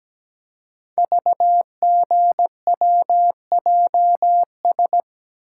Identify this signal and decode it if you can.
Morse code, heard as a keyed tone, about 17 wpm, 700 Hz. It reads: VGWJS
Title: Code Groups - 5 Letters - 17wpm